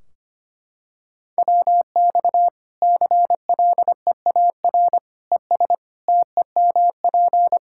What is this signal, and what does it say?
Morse code, heard as a keyed tone, about 25 wpm, 700 Hz. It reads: WX CLEAR ES TEMP
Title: Practice QSO - Exchange 2 - Info - 25wpm